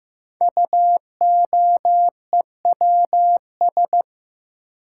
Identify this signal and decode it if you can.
Morse code, heard as a keyed tone, about 15 wpm, 700 Hz. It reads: UOEWS
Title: Code Groups - 5 Letters - 15wpm